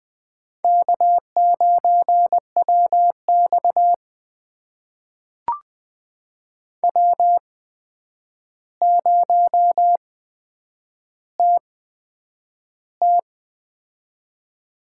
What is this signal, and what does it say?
Morse code, heard as a keyed tone, about 20 wpm, spaced as for 9 wpm, 700 Hz. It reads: K9WX W0TT
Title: USA Call Signs - 4 Characters - 8x Character Spacing - 20wpm